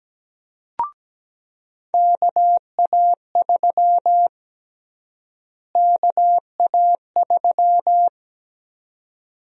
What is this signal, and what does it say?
Morse code, heard as a keyed tone, about 17 wpm, 700 Hz. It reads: KA3 KA3
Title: USA Call Sign Prefixes - Repeated 2x - 17wpm